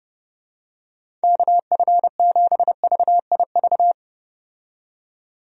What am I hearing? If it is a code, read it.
Morse code, heard as a keyed tone, about 30 wpm, 700 Hz. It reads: KF7VIV
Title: USA Call Signs - 6 Characters - 30wpm